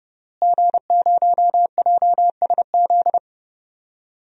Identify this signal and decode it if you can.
Morse code, heard as a keyed tone, about 32 wpm, 700 Hz. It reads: G0JSZ
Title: DXCC-100 Call Signs - 5 Characters - Repeated 2x - 30wpm